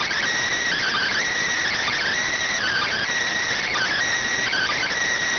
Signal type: Emergency Alert System (SAME) tones

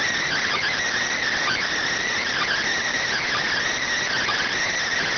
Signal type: Emergency Alert System (SAME) tones

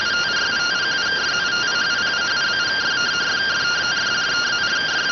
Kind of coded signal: two-tone teleprinter signal

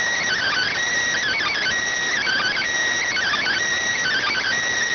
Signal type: Emergency Alert System (SAME) tones

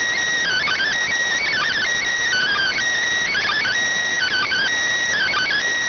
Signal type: Emergency Alert System (SAME) tones

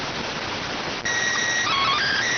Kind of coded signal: SSTV (slow-scan television)

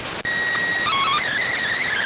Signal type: SSTV (slow-scan television)

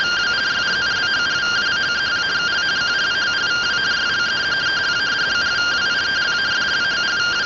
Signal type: two-tone teleprinter signal

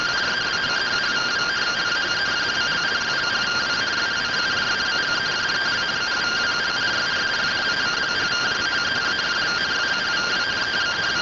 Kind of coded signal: two-tone teleprinter signal

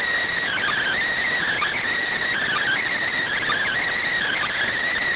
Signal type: Emergency Alert System (SAME) tones

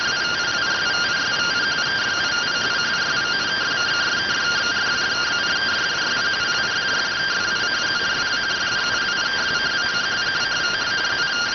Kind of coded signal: two-tone teleprinter signal

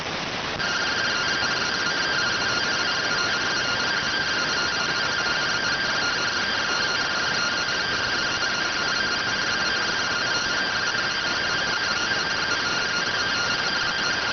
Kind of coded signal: two-tone teleprinter signal

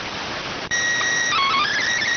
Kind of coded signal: SSTV (slow-scan television)

